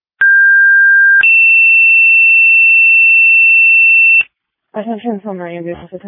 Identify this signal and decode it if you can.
Two-tone page, heard as a one-second tone, then a three-second tone, about 1600 then 2700 Hz, with speech. It is heard from 0.2 to 4.2 s.